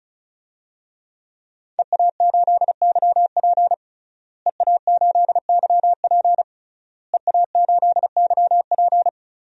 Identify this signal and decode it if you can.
Morse code, heard as a keyed tone, about 34 wpm, 700 Hz. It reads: EA8YP EA8YP EA8YP
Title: DXCC-100 Call Signs - 5 Characters - Repeated 3x - 35wpm